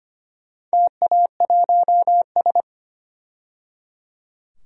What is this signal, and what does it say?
Morse code, heard as a keyed tone, about 25 wpm, 700 Hz. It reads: TA1S
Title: DXCC-100 Call Signs - 4 Characters - 25wpm